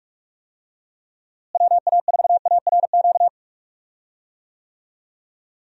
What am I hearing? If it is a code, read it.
Morse code, heard as a keyed tone, about 46 wpm, 700 Hz. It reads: WA4ARQ
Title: USA Call Signs - 6 Characters - Repeated 3x - 45wpm